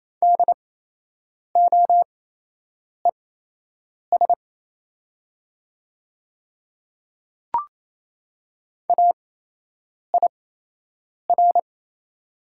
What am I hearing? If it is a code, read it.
Morse code, heard as a keyed tone, about 28 wpm, spaced as for 8 wpm, 700 Hz. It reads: DOES AIR
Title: 8x Character Spacing - Top 200 Words - 28wpm